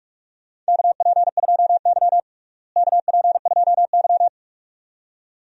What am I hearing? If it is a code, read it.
Morse code, heard as a keyed tone, about 42 wpm, 700 Hz. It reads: KP2Y KP2Y
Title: DXCC-100 Call Signs - 4 Characters - Repeated 2x - 45wpm